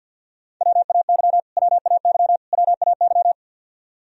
Morse code, heard as a keyed tone, about 47 wpm, 700 Hz. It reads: WAY WAY WAY
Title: Top 100 Words Repeated 50wpm